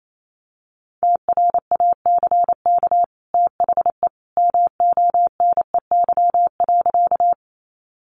Morse code, heard as a keyed tone, about 28 wpm, 700 Hz. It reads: TRACK THE MONEY.
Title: Sentences from Top 1100 Words 28wpm